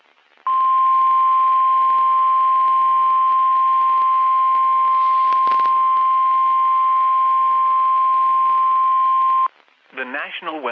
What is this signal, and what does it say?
Emergency Alert System (SAME) tones (the 1050 Hz attention tone), with speech. It is heard from 0.4 to 9.5 s.
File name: eas messages / EAS WXM31 Elmira